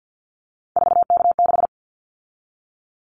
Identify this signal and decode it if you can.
Morse code, heard as a keyed tone, about 50 wpm, 700 Hz. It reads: 4K6
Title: Callsign Prefixes for Top 100 DXCC - 50wpm